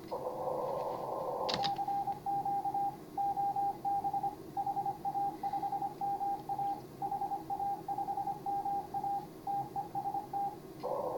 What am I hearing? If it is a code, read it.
Morse code, heard as a keyed tone, about 25 wpm, 810 Hz, with radio noise. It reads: CQ OB HA5MA HA5MA TEST
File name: CW / HA5MA RHOB HANG CW 2017